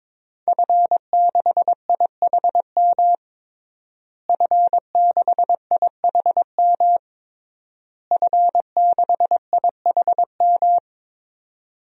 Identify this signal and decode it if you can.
Morse code, heard as a keyed tone, about 22 wpm, 700 Hz. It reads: F6IHM F6IHM F6IHM
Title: DXCC-100 Call Signs - 5 Characters - Repeated 3x - 22wpm